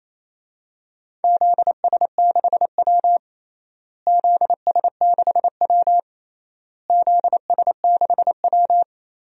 Morse code, heard as a keyed tone, about 28 wpm, 700 Hz. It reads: ZS6W ZS6W ZS6W
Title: DXCC-100 Call Signs - 4 Characters - Repeated 3x - 28wpm